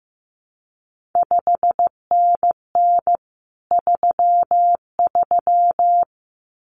Morse code, heard as a keyed tone, about 15 wpm, 700 Hz. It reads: 5NN 33
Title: CQ WW DX Contest Exchange Encore - 15wpm